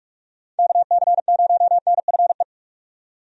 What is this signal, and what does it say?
Morse code, heard as a keyed tone, about 41 wpm, 700 Hz. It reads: KC0NFE
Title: USA Call Signs - 6 Characters - 45wpm